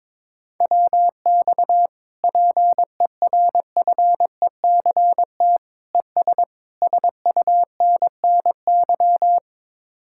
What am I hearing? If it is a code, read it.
Morse code, heard as a keyed tone, about 22 wpm, 700 Hz. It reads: WX PERFECT ES SUNNY
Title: Speed-Racing Common QSO Phrases - 22wpm